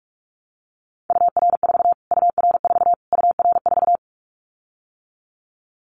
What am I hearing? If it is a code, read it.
Morse code, heard as a keyed tone, about 44 wpm, 700 Hz. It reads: UR4 UR4 UR4
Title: Callsign Prefixes for Top 25 DXCC - Repeated 3x - 45wpm